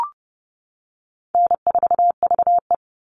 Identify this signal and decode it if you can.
Morse code, heard as a keyed tone, about 30 wpm, 700 Hz. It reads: N4VE
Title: All Cut Numbers - 4-Digit 30wpm